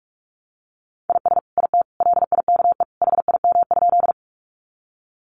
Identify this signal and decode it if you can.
Morse code, heard as a keyed tone, about 44 wpm, 700 Hz. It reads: IS IT LIKE HIM?
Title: Sentences from Top 100 Words 45wpm